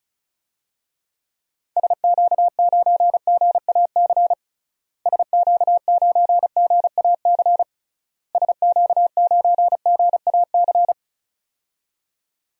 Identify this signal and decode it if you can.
Morse code, heard as a keyed tone, about 36 wpm, 700 Hz. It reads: SQ9GAC SQ9GAC SQ9GAC
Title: DXCC-100 Call Signs - 6 Characters - Repeated 3x - 35wpm